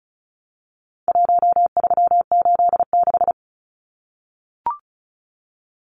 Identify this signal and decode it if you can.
Morse code, heard as a keyed tone, about 34 wpm, 700 Hz. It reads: 1386
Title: Numbers - Contest Style - 4-Digit 35wpm